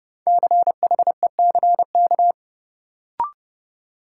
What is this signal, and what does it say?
Morse code, heard as a keyed tone, about 31 wpm, 700 Hz. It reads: CHECK
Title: Sentences from Top 1000 Words - Review Part 3 - New Word Drill - 30wpm